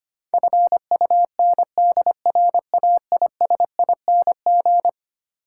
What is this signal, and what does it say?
Morse code, heard as a keyed tone, about 25 wpm, 700 Hz. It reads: FUNDRAISING
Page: morse-code-ninja-practice-25wpm